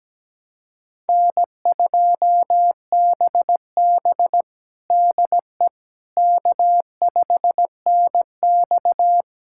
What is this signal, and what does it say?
Morse code, heard as a keyed tone, about 17 wpm, 700 Hz. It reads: N2BB DE K5NX
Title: USA Callsign DE Callsign - 4 Characters - 17wpm